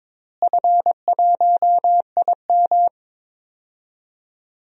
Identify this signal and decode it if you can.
Morse code, heard as a keyed tone, about 21 wpm, 700 Hz. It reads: F1IM
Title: DXCC-100 Call Signs - 4 Characters - Repeated 3x - 22wpm